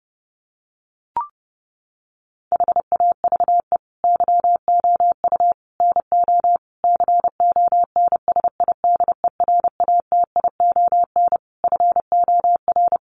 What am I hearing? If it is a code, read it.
Morse code, heard as a keyed tone, about 30 wpm, 700 Hz. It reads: HAVE YOU NO CONSIDERATION FOR
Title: Sentences from Top 1500 Words 30wpm